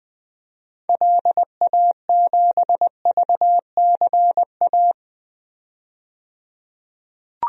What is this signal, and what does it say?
Morse code, heard as a keyed tone, about 20 wpm, 700 Hz. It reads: LA7VCA
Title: DXCC-100 Call Signs - 6 Characters - 20wpm